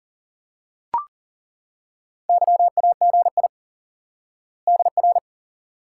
Morse code, heard as a keyed tone, about 40 wpm, 700 Hz. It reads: YAGI DR
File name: IQE Course 3 - Lesson 89 - YAGI - Two Elements - 40wpm